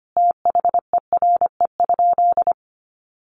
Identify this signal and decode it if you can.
Morse code, heard as a keyed tone, about 25 wpm, 700 Hz. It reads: THERE?
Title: Sentences from Top 100 Words 25wpm